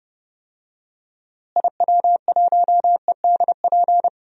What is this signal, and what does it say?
Morse code, heard as a keyed tone, about 30 wpm, 700 Hz. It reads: IW1EDP
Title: DXCC-100 Call Signs - 6 Characters - Repeated 3x - 30wpm